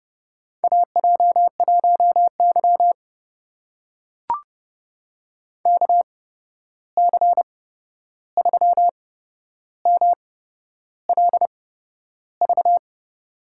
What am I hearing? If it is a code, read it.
Morse code, heard as a keyed tone, about 30 wpm, spaced as for 11 wpm, 700 Hz. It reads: AJ1Y KC3MLV
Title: USA Call Signs - 8x Character Spacing - 30wpm